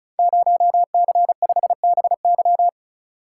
Morse code, heard as a keyed tone, about 36 wpm, 700 Hz. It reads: 0C5BY
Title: Code Groups - 5 Letter:Numbers - 35wpm